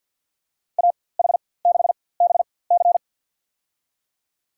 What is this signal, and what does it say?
Morse code, heard as a keyed tone, about 53 wpm, spaced as for 24 wpm, 700 Hz. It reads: AH6BC
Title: USA Call Signs - 5 Characters - 4x Character Spacing - 50wpm